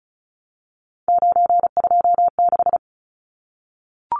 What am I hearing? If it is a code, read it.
Morse code, heard as a keyed tone, about 34 wpm, 700 Hz. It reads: 926
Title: Numbers - 3-Digit 35wpm